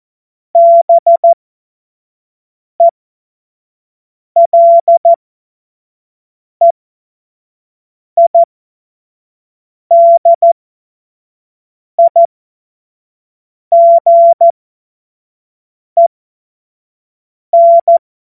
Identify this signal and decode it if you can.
Morse code, heard as a keyed tone, about 14 wpm, spaced as for 5 wpm, 680 Hz. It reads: BELEIDIGEN
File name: Morsekurs / Witze / Witz 04. April 2023 25 F70